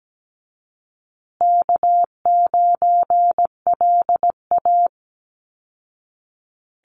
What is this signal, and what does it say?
Morse code, heard as a keyed tone, about 17 wpm, 700 Hz. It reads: K9LA
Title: Easy Call Signs Repeated 3x - 17wpm